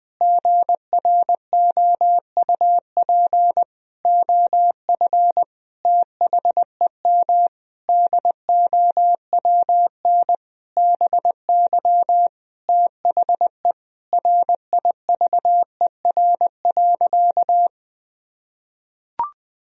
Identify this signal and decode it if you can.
Morse code, heard as a keyed tone, about 20 wpm, 700 Hz. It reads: GROUP OF THEM DOWN BY THE RIVER.
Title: Sentences from Top 1000 Words - Review Part 1 - 20wpm